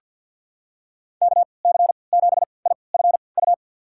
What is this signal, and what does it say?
Morse code, heard as a keyed tone, about 49 wpm, spaced as for 28 wpm, 700 Hz. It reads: KC7IFU
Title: USA Call Signs - 6 Characters - 3x Character Spacing - 50wpm